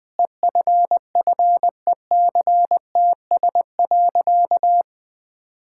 Morse code, heard as a keyed tone, about 20 wpm, 700 Hz. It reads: EFFECTS.
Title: Sentences with 5-Word Phrases - Spoken Beginning - 20wpm